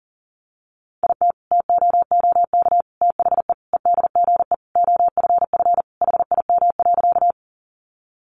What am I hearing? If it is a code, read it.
Morse code, heard as a keyed tone, about 40 wpm, 700 Hz. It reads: IT TOOK THE EDGE OFF HIM.
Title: Sentences from Top 1300 Words 40wpm